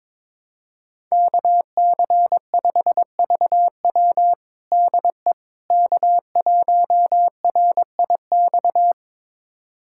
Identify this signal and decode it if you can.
Morse code, heard as a keyed tone, about 22 wpm, 700 Hz. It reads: KC5VW DE K1RIX
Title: USA Callsign DE Callsign 22wpm